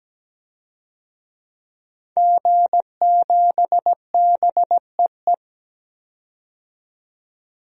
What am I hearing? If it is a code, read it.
Morse code, heard as a keyed tone, about 17 wpm, 700 Hz. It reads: G7BEE
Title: DXCC-100 Call Signs - 5 Characters - 17wpm